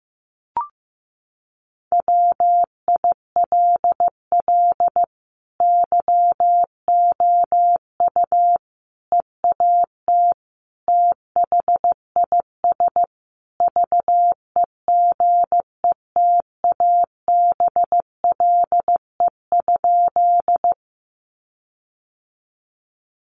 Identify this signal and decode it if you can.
Morse code, heard as a keyed tone, about 15 wpm, 700 Hz. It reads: WILL YOU EAT THIS VEGETABLE?
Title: Sentences from Top 1300 Words 15wpm